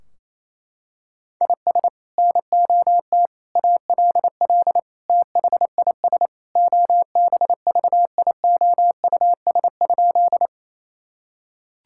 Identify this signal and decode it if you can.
Morse code, heard as a keyed tone, about 28 wpm, 700 Hz. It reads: IS NOT ALL THIS OBVIOUS?
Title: Sentences from Top 1600 Words 28wpm